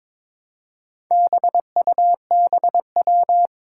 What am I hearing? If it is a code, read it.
Morse code, heard as a keyed tone, about 22 wpm, 700 Hz. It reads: BUBW
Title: Four Challenging Letter-Number Combinations 22wpm